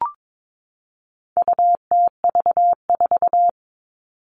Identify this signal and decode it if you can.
Morse code, heard as a keyed tone, about 22 wpm, 700 Hz. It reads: UTV4
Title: All Cut Numbers - 4-Digit 22wpm